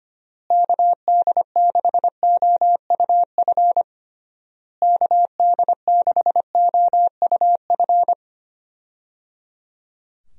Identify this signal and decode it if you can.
Morse code, heard as a keyed tone, about 25 wpm, 700 Hz. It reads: KD6OUF KD6OUF
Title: USA Call Signs - 6 Characters - Repeated 3x - 25wpm